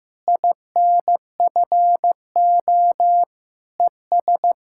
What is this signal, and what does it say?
Morse code, heard as a keyed tone, about 15 wpm, 700 Hz. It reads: INFO ES
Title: K1USN SST QSO Practice 15wpm